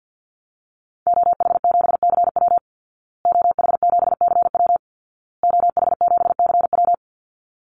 Morse code, heard as a keyed tone, about 50 wpm, 700 Hz. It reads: OH7CW OH7CW OH7CW
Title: Call Signs Repeated 2x - 50wpm